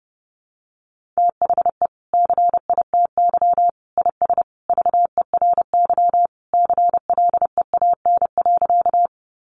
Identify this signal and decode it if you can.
Morse code, heard as a keyed tone, about 30 wpm, 700 Hz. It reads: THE CITY IS VERY CLEAN.
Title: Easy Phrases 30wpm